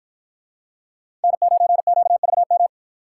Morse code, heard as a keyed tone, about 53 wpm, 700 Hz. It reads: N9QVM